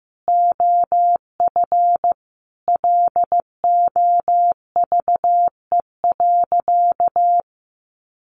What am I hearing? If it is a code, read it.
Morse code, heard as a keyed tone, about 15 wpm, 700 Hz. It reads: OF LOVE.
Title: Sentences from Top 1400 Words 15wpm